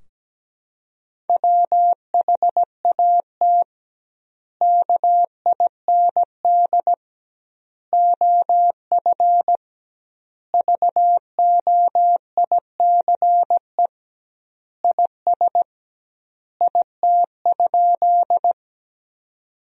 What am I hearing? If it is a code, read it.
Morse code, heard as a keyed tone, about 17 wpm, 700 Hz. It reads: WHAT KIND OF VOICE IS IT?
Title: 2x Word-Spacing - Sentences from Top 500 Words - 17wpm